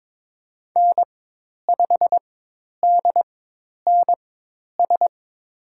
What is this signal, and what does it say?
Morse code, heard as a keyed tone, about 23 wpm, spaced as for 10 wpm, 700 Hz. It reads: N5DNS
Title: USA Call Signs - 5 Characters - 4x Character Spacing - 22wpm